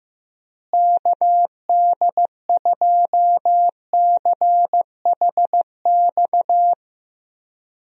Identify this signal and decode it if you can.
Morse code, heard as a keyed tone, about 15 wpm, 700 Hz. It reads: KD2CHX
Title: USA Call Signs - 6 Characters - Repeated 2x - 15wpm